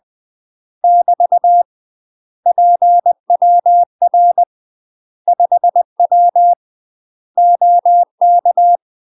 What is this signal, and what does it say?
Morse code, heard as a keyed tone, about 20 wpm, 700 Hz. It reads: <BT> PWR 5W OK
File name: UK MASTER QSOs TRAINING / 20WPM / UK MASTER QSOs 0052